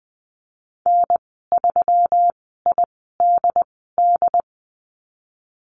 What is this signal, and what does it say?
Morse code, heard as a keyed tone, about 20 wpm, spaced as for 15 wpm, 700 Hz. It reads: N3IDD
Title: 179 Lesson 29g - Slash - Call Signs 20wpm